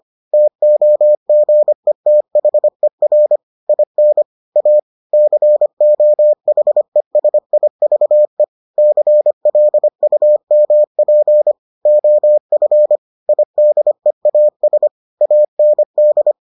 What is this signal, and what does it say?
Morse code, heard as a keyed tone, about 25 wpm, 600 Hz. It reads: TOGETHER IN A COHESIVE CLUMP OF IDEAS AND